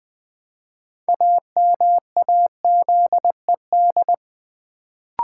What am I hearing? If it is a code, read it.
Morse code, heard as a keyed tone, about 20 wpm, 700 Hz. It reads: AMAZED
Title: Build A Word - 3 Letters At A Time - 6-Letter Words - 20wpm